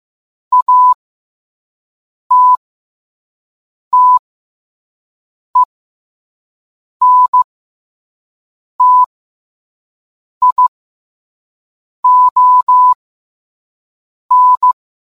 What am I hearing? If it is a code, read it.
Morse code, heard as a keyed tone, about 15 wpm, spaced as for 5 wpm, 1000 Hz. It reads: ATTENTION